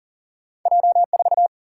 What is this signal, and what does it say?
Morse code, heard as a keyed tone, about 38 wpm, 700 Hz. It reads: J4
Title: Single-Letter Single-Number 40wpm